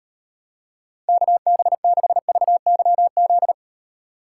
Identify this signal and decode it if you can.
Morse code, heard as a keyed tone, about 31 wpm, 700 Hz. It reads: KB6VYZ